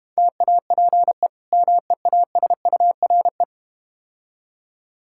Morse code, heard as a keyed tone, about 32 wpm, 700 Hz. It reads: TAPE MEASURE